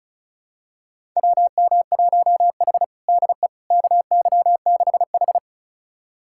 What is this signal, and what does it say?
Morse code, heard as a keyed tone, about 36 wpm, 700 Hz. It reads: WM1H DE KY6H
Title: USA Callsign DE Callsign - 4 Characters - 35wpm